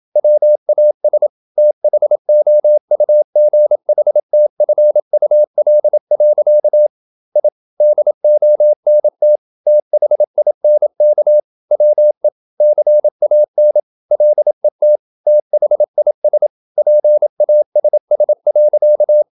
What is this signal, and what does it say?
Morse code, heard as a keyed tone, about 27 wpm, 600 Hz. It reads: WAS THOUGHTFUL. I DONT THINK WE CAN LET THIS PASS.
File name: Never 0027wpm